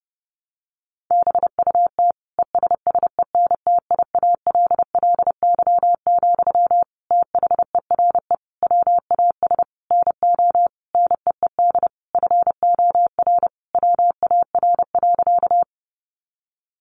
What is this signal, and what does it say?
Morse code, heard as a keyed tone, about 30 wpm, 700 Hz. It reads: BUT ESSENTIALLY, THERE WAS NO NEED FOR WAR.
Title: Sentences from Top 1300 Words 30wpm